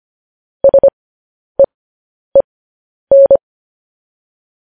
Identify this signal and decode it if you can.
Morse code, heard as a keyed tone, about 25 wpm, spaced as for 10 wpm, 550 Hz. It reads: SEEN
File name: podcast / News headlines-1x-25x10